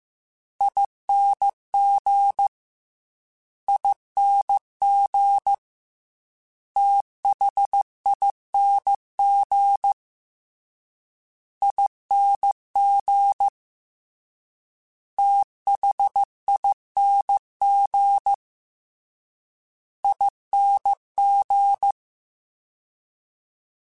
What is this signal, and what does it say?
Morse code, heard as a keyed tone, about 15 wpm, 800 Hz. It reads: ING ING THING ING THING ING
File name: Special INGs / Special INGs-mp3s / SlowToFaster ing @13